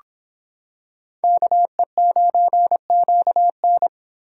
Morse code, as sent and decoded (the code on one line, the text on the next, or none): -.- . ----. --.- -.
KE9QN